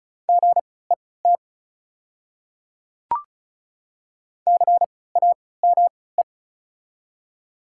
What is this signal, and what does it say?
Morse code, heard as a keyed tone, about 34 wpm, spaced as for 20 wpm, 690 Hz. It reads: GET CAME
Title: 3x Character Spacing - Top 200 Words - 35wpm